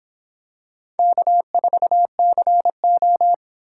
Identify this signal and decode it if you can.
Morse code, heard as a keyed tone, about 25 wpm, 700 Hz. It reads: K4CO